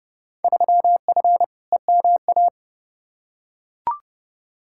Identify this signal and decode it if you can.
Morse code, heard as a keyed tone, about 32 wpm, 700 Hz. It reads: 3F EMA
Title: Field Day Exchanges Encore - 30wpm